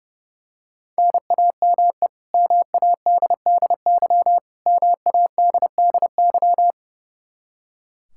Morse code, heard as a keyed tone, about 30 wpm, 700 Hz. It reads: NAME MADDY MADDY
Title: Practice QSO - Exchange 1 - Beginning - 30wpm